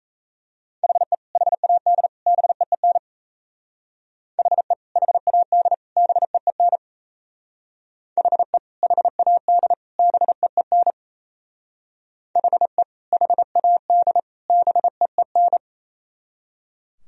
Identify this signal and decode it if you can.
Morse code, heard as a keyed tone, about 35 wpm, 700 Hz. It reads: HE HAD BEEN HE HAD BEEN HE HAD BEEN HE HAD BEEN
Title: Speed-Racing - Sets of 3 Words 28wpm